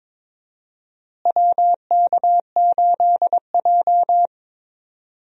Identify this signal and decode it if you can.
Morse code, heard as a keyed tone, about 22 wpm, 700 Hz. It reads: WK8J
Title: USA Call Signs - 4 Characters - Repeated 2x - 22wpm